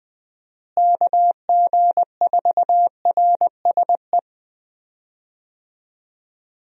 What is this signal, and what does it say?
Morse code, heard as a keyed tone, about 20 wpm, 700 Hz. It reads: KG4RSE
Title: USA Call Signs - 6 Characters - Repeated 2x - 20wpm